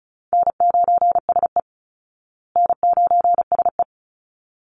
Morse code, heard as a keyed tone, about 34 wpm, 700 Hz. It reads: N9SE N9SE
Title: Easy Call Signs Repeated 2x - 35wpm